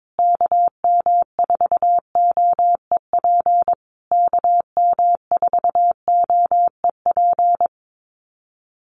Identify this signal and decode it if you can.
Morse code, heard as a keyed tone, about 22 wpm, 700 Hz. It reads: KM4OEP KM4OEP
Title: USA Call Signs Repeated 3x 22wpm